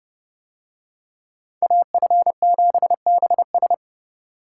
Morse code, heard as a keyed tone, about 30 wpm, 700 Hz. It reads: AF7BS